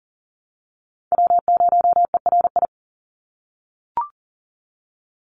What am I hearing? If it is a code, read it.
Morse code, heard as a keyed tone, about 40 wpm, 700 Hz. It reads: W0ERI
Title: NA SOTA Chaser Callsigns 40wpm